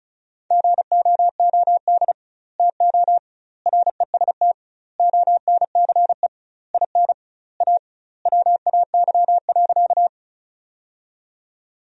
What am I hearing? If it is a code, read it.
Morse code, heard as a keyed tone, about 36 wpm, 700 Hz. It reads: GOOD TO REST ONCE IN A WAY.
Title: 2x Word-Spacing - Sentences from Top 500 Words - 35wpm